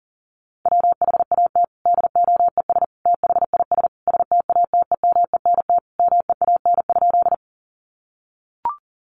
Morse code, heard as a keyed tone, about 40 wpm, 700 Hz. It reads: WHAT DOES THIS STATEMENT MEAN?
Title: Sentences from Top 1200 Words 40wpm